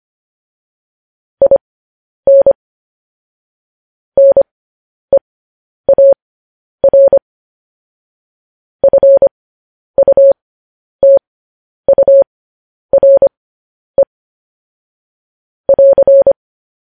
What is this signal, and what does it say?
Morse code, heard as a keyed tone, about 25 wpm, spaced as for 10 wpm, 550 Hz. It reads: IN NEAR FUTURE <AR>
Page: news headlines in morse code at 10 wpm